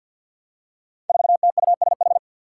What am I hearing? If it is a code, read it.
Morse code, heard as a keyed tone, about 47 wpm, 700 Hz. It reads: 4TUSH
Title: Five Challenging Letter-Number Combinations 50wpm